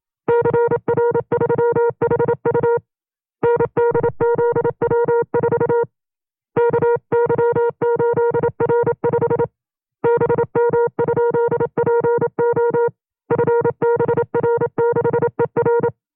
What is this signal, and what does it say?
Morse code, heard as a keyed tone, about 27 wpm, 470 Hz. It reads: CR3HU NDZW4 KY8R5 BM?PO FBR6ER
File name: 25WPM